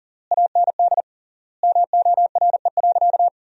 Morse code, heard as a keyed tone, about 40 wpm, 700 Hz. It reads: AND MORE.
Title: 3x Word-Spacing - Sentences from Top 500 Words - 40wpm